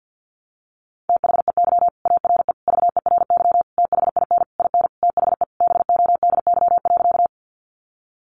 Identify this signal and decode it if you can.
Morse code, heard as a keyed tone, about 48 wpm, 700 Hz. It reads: THEY ARE VERY THIN IN THE BODY.
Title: Sentences from Top 1200 Words 50wpm